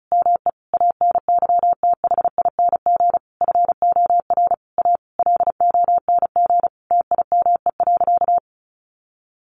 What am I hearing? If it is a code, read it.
Morse code, heard as a keyed tone, about 34 wpm, 700 Hz. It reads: ME ANYTHING FOR A LONG TIME.
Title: Sentences from Top 1000 Words 35wpm